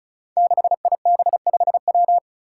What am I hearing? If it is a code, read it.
Morse code, heard as a keyed tone, about 36 wpm, 700 Hz. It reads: 6IB5W
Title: Five Challenging Letter-Number Combinations 35wpm